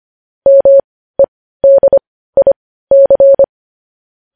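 Morse code, heard as a keyed tone, about 25 wpm, spaced as for 15 wpm, 550 Hz. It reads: MEDIC
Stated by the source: Morse code news headlines at 15 wpm.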